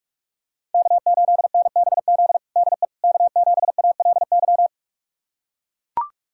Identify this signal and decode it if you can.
Morse code, heard as a keyed tone, about 41 wpm, 700 Hz. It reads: K8NBZ DE K7ALY
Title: USA Callsign DE Callsign - 5 Characters - 45wpm